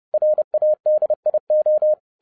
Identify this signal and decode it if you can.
Morse code, heard as a keyed tone, about 30 wpm, 610 Hz. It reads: RADIO